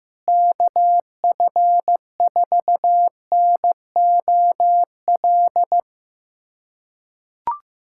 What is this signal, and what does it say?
Morse code, heard as a keyed tone, about 15 wpm, 700 Hz. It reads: KF4NOL
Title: USA Callsign DE Callsign - 6 Characters - 15wpm